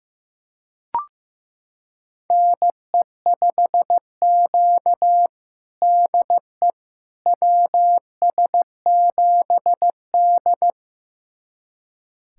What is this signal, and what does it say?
Morse code, heard as a keyed tone, about 15 wpm, 700 Hz. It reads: NE5Q DE WS7D
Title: USA Callsign DE Callsign - 4 Characters - 15wpm